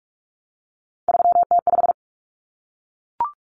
Morse code, heard as a keyed tone, about 44 wpm, 700 Hz. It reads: 3T5
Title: Common Cut Numbers - 3-Digit 45wpm